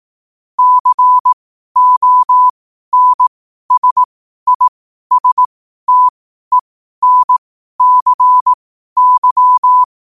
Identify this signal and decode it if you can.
Morse code, heard as a keyed tone, about 18 wpm, spaced as for 12 wpm, 1000 Hz. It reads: CONSISTENCY
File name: qotdfiles / 12wpm